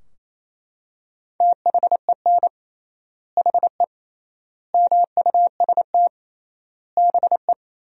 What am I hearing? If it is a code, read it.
Morse code, heard as a keyed tone, about 28 wpm, 700 Hz. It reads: THEN HE MUST BE
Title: 3x Word-Spacing - Sentences from Top 500 Words - 28wpm